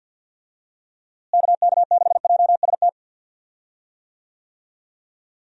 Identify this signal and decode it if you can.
Morse code, heard as a keyed tone, about 52 wpm, 700 Hz. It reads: KK6JST
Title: USA Call Signs - 6 Characters - Repeated 2x - 50wpm